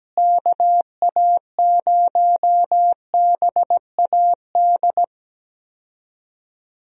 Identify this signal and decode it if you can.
Morse code, heard as a keyed tone, about 17 wpm, 700 Hz. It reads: KA0BAD
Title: USA Call Signs - 6 Characters - Repeated 2x - 17wpm